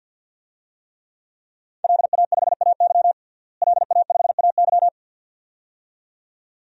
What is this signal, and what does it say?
Morse code, heard as a keyed tone, about 46 wpm, 700 Hz. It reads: LA5AY LA5AY
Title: DXCC-100 Call Signs - 5 Characters - Repeated 2x - 50wpm